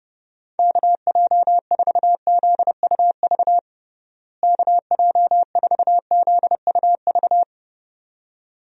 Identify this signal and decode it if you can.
Morse code, heard as a keyed tone, about 30 wpm, 700 Hz. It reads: KJ4ZUV KJ4ZUV
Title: USA Call Signs - 6 Characters - Repeated 2x - 30wpm